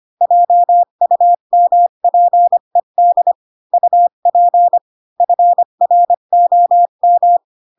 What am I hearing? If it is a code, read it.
Morse code, heard as a keyed tone, about 25 wpm, 700 Hz. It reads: JUMPED UP FROM
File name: Pinocchio / Pinocchio ch8 20 25 ik1hge